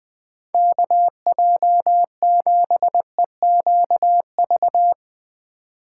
Morse code, heard as a keyed tone, about 19 wpm, 700 Hz. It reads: KJ7EQV
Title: Speed Racing - USA 2x3 Call Signs 17wpm